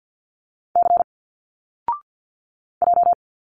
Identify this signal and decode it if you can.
Morse code, heard as a keyed tone, about 52 wpm, 700 Hz. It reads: C J
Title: Single Letters 50wpm